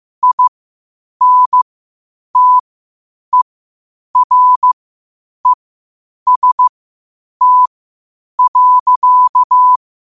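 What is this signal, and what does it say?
Morse code, heard as a keyed tone, about 15 wpm, spaced as for 8 wpm, 1000 Hz. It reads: INTEREST.
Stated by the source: Morse code quotes at 8 words per minute from QOTD at 8 WPM.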